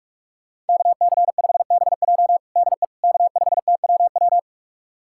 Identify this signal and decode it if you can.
Morse code, heard as a keyed tone, about 46 wpm, 700 Hz. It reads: KC5BJ DE K5TWW
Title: USA Callsign DE Callsign - 5 Characters - 45wpm